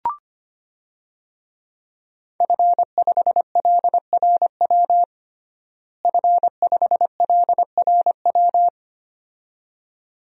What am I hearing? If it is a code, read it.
Morse code, heard as a keyed tone, about 25 wpm, 700 Hz. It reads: F5LRW F5LRW
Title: DXCC-100 Call Signs - 5 Characters - Repeated 2x - 25wpm